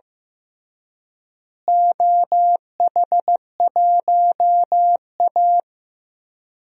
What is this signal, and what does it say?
Morse code, heard as a keyed tone, about 15 wpm, 700 Hz. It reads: OH1A
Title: DXCC-100 Call Signs - 4 Characters - Repeated 3x - 15wpm